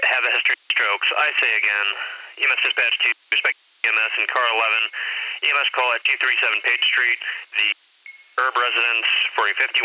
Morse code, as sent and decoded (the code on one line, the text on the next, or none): none